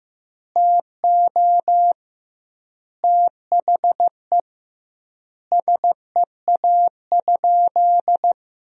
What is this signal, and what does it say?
Morse code, heard as a keyed tone, about 15 wpm, 700 Hz. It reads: TO THE SEA?
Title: 2x Word-Spacing - Sentences from Top 300 Words - 15wpm